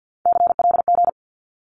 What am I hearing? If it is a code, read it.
Morse code, heard as a keyed tone, about 50 wpm, 700 Hz. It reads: CLG
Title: Abbreviations 50wpm